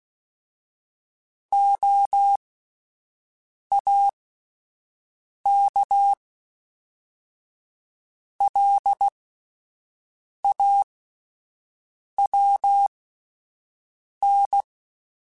Morse code, heard as a keyed tone, about 16 wpm, spaced as for 5 wpm, 800 Hz. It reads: OAKLAWN